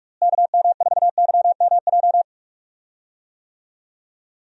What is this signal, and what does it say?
Morse code, heard as a keyed tone, about 45 wpm, 700 Hz. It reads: KM4YMJ
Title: USA Call Signs - 6 Characters - Repeated 2x - 45wpm